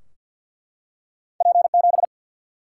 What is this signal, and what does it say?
Morse code, heard as a keyed tone, about 49 wpm, 700 Hz. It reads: P 7
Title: Single-Letter Single-Number 50wpm